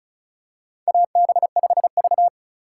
Morse code, heard as a keyed tone, about 34 wpm, 700 Hz. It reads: AB5V
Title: USA Call Signs - 4 Characters - 2x Character Spacing - 35wpm